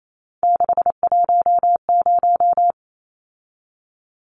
Morse code, heard as a keyed tone, about 28 wpm, 700 Hz. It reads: 610
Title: Numbers - 3-Digit 28wpm